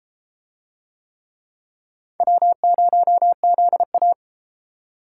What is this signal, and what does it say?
Morse code, heard as a keyed tone, about 30 wpm, 700 Hz. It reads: W0ZA